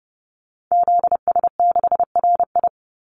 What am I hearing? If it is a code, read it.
Morse code, heard as a keyed tone, about 30 wpm, 700 Hz. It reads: ZS6RI
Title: Call Signs Repeated 3x - 30wpm